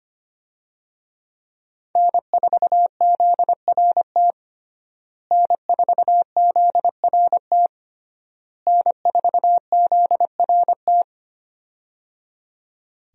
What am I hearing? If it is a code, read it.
Morse code, heard as a keyed tone, about 25 wpm, 700 Hz. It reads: N4ZRT N4ZRT N4ZRT
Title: USA Call Signs - 5 Characters - Repeated 3x - 25wpm